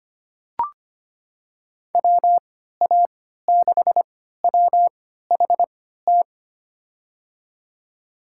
Morse code, heard as a keyed tone, about 25 wpm, spaced as for 14 wpm, 700 Hz. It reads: WA6WHT
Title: USA Call Signs - 6 Characters - 3x Character Spacing - 25wpm